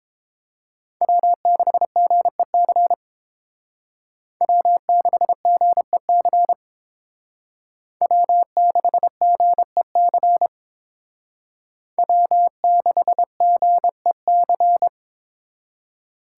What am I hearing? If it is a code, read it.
Morse code, heard as a keyed tone, about 28 wpm, 700 Hz. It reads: W6GEC W6GEC W6GEC W6GEC